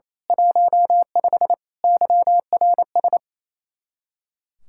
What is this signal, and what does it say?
Morse code, heard as a keyed tone, about 26 wpm, 700 Hz. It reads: 15 YRS
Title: Speed-Racing Common QSO Phrases - 28wpm